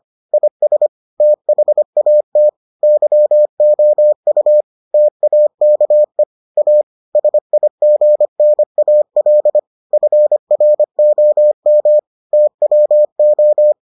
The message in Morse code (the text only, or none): IS THAT YOU TAKE A SIGNAL FROM TWO